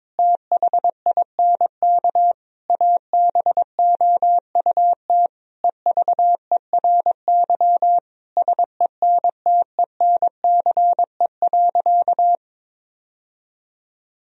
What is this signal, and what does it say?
Morse code, heard as a keyed tone, about 22 wpm, 700 Hz. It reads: THINK ABOUT EVERY SENTENCE.
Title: Sentences from Top 1000 Words - Review Part 1 - 22wpm